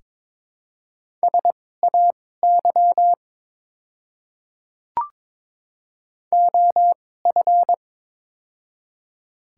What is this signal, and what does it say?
Morse code, heard as a keyed tone, about 23 wpm, spaced as for 16 wpm, 700 Hz. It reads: SAY OF
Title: 2x Character Spacing - Top 200 Words - 22wpm